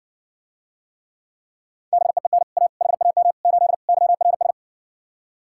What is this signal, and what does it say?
Morse code, heard as a keyed tone, about 69 wpm, 700 Hz. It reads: BEEN A HAM 8 YRS